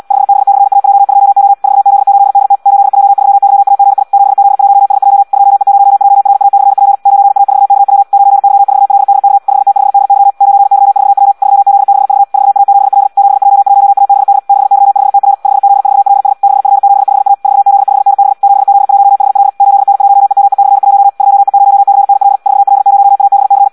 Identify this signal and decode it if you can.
Morse code, heard as a keyed tone, about 74 wpm, 800 Hz. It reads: 5B/AJ2O 4Z8DT 8J3YAGI 7Z1SJ 2E0JRA/P 9A4KW 9Y4LDK 5T5DY 9Y4W 3Z6V 4A7L 7J1ABD 6Y5TR 5Z4LS 6F75A 4Z5NB 7Z1HL 8N1EME/1 2E0CNL 4L0ABC
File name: Audio practica CW / qrq-tx-78 2